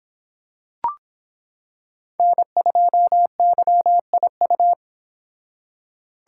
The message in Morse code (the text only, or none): N2YIU